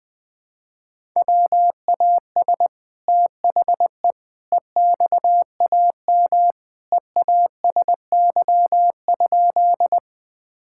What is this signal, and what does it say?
Morse code, heard as a keyed tone, about 21 wpm, 700 Hz. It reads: WAS THE EXAM EASY?
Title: Sentences from Top 1000 Words - Review Part 5 - 20wpm